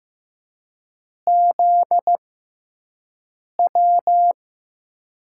Morse code, heard as a keyed tone, about 15 wpm, spaced as for 5 wpm, 700 Hz. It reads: ZW
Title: USA Call Signs - 6 Characters - 6x Character Spacing - 15wpm